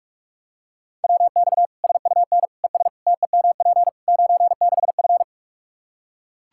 Morse code, heard as a keyed tone, about 46 wpm, 700 Hz. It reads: WX SUN ES TEMP 96F
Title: Practice QSO - Exchange 2 - Info - 45wpm